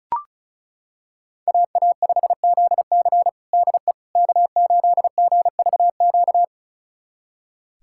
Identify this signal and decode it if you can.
Morse code, heard as a keyed tone, about 36 wpm, 700 Hz. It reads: AA5ZC DE K8GVQ
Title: USA Callsign DE Callsign - 5 Characters - 35wpm